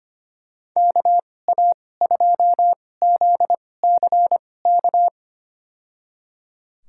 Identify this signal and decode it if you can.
Morse code, heard as a keyed tone, about 25 wpm, spaced as for 18 wpm, 700 Hz. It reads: KA2ZCK
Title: USA Call Signs - 6 Characters - 2x Character Spacing - 25wpm